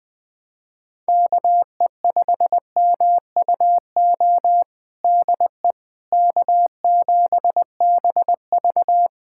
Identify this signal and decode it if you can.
Morse code, heard as a keyed tone, about 21 wpm, 700 Hz. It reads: KE5MUO DE K7BV
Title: K1USN SST QSO Practice 20wpm